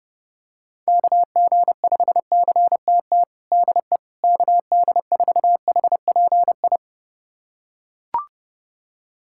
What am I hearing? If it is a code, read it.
Morse code, heard as a keyed tone, about 29 wpm, 700 Hz. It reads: KG5CTT DE KD4HPI
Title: USA Callsign DE Callsign 30wpm